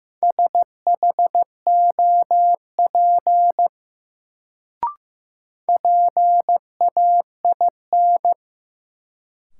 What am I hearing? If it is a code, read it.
Morse code, heard as a keyed tone, about 15 wpm, 700 Hz. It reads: SHOP PAIN
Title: Sentences from Top 1600 Words - New Word Drill - 15wpm